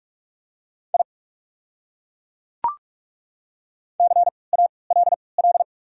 Morse code, heard as a keyed tone, about 42 wpm, 690 Hz. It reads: I C A L L
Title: 3x Character Spacing - Top 200 Words - 45wpm